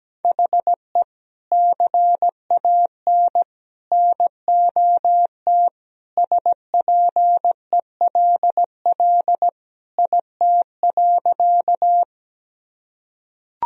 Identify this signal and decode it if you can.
Morse code, heard as a keyed tone, about 17 wpm, 700 Hz. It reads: HE CAN NOT SPELL IT.
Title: Sentences from Top 1000 Words - Review Part 1 - 17wpm